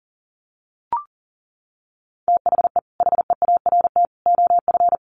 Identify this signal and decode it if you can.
Morse code, heard as a keyed tone, about 40 wpm, 700 Hz. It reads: THE HEART OF
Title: Sets of 3 Words 40wpm